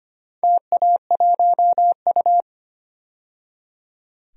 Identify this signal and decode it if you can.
Morse code, heard as a keyed tone, about 25 wpm, 700 Hz. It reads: TA1U
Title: DXCC-100 Call Signs - 4 Characters - 25wpm